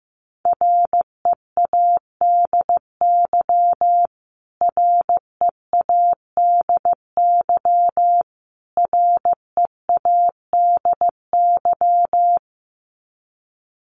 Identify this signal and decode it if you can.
Morse code, heard as a keyed tone, about 14 wpm, 700 Hz. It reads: READY READY READY
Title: Top 500 Words Repeated 15wpm